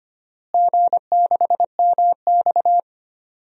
Morse code, heard as a keyed tone, about 25 wpm, 700 Hz. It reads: G6MX
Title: DXCC-100 Call Signs - 4 Characters - Repeated 3x - 25wpm